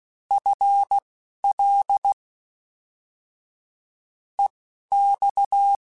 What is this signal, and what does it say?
Morse code, heard as a keyed tone, about 16 wpm, spaced as for 12 wpm, 800 Hz. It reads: FL EX
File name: FL / FL mp3s / Fast FL @10